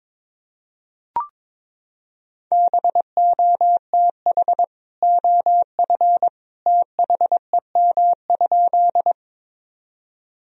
Morse code, heard as a keyed tone, about 22 wpm, 700 Hz. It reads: BOTH OF THEM?
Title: Sentences from Top 1000 Words - Review Part 1 - 22wpm